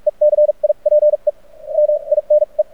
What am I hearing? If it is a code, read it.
Morse code, heard as a keyed tone, about 48 wpm, spaced as for 37 wpm, 600 Hz. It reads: ECIPE MINE
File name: archived / CRSnet